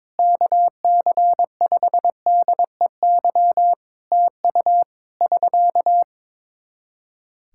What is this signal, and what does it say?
Morse code, heard as a keyed tone, about 22 wpm, 700 Hz. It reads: KC5DEY TU <SK>
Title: Practice QSO - Exchange 4 - Ending - 22wpm